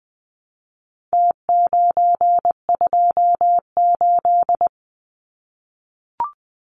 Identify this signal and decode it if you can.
Morse code, heard as a keyed tone, about 20 wpm, 700 Hz. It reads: T928
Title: Numbers - Contest Style - 4-Digit 20wpm